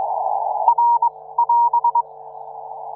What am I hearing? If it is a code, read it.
Morse code, heard as a keyed tone, about 22 wpm, 940 Hz. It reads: R <AS>